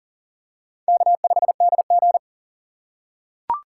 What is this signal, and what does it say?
Morse code, heard as a keyed tone, about 40 wpm, 700 Hz. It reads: K5DG
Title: USA Call Signs - 2x Character Spacing - 40wpm